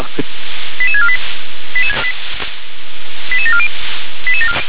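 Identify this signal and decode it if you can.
Five-tone selective call (ZVEI), heard as quick runs of five tones, tones of about 70 ms. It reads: 88648 88740
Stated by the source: ZVEI five-tone call 88152 received by SDR.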